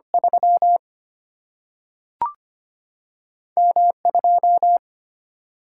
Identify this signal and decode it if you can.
Morse code, heard as a keyed tone, about 25 wpm, 700 Hz. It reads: 3 M 2
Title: Single-Letter Single-Number 25wpm